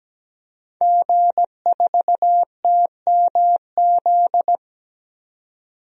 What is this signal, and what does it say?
Morse code, heard as a keyed tone, about 17 wpm, 700 Hz. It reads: G4TMZ
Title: DXCC-100 Call Signs - 5 Characters - 17wpm